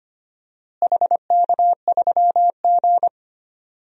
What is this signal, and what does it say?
Morse code, heard as a keyed tone, about 25 wpm, 700 Hz. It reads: HK3G
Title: DXCC-100 Call Signs - 4 Characters - Repeated 3x - 25wpm